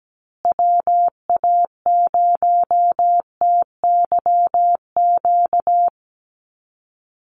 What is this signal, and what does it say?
Morse code, heard as a keyed tone, about 17 wpm, 700 Hz. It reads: WA0TYQ
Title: USA Call Signs Repeated 2x 17wpm